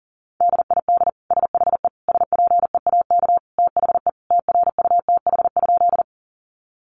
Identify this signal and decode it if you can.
Morse code, heard as a keyed tone, about 40 wpm, 700 Hz. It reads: DID SHE SPEAK THE TRUTH?
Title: Sentences from Top 1100 Words 40wpm